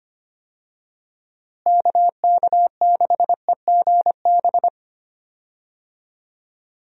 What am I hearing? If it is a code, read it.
Morse code, heard as a keyed tone, about 25 wpm, 700 Hz. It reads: KK6EGB
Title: USA Call Signs - 6 Characters - Repeated 3x - 25wpm